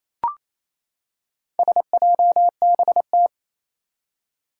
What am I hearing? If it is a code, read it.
Morse code, heard as a keyed tone, about 27 wpm, 700 Hz. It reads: SJBT
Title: Four Challenging Letter-Number Combinations 28wpm